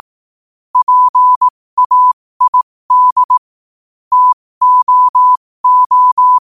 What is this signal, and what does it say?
Morse code, heard as a keyed tone, about 18 wpm, spaced as for 15 wpm, 1000 Hz. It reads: PAID TOO
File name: qotdfiles / 15wpm